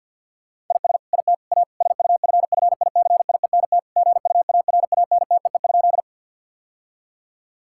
Morse code, heard as a keyed tone, about 50 wpm, 700 Hz. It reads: IS IT A SUFFICIENT GUARANTEE?
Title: Sentences from Top 1600 Words 50wpm